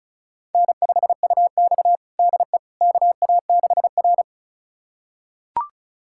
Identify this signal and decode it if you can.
Morse code, heard as a keyed tone, about 36 wpm, 700 Hz. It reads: N5UX DE KA6R
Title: USA Callsign DE Callsign - 4 Characters - 35wpm